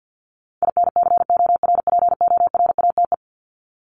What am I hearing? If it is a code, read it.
Morse code, heard as a keyed tone, about 50 wpm, 700 Hz. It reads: INCORPORATE
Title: Top 1751-2000 Words 50wpm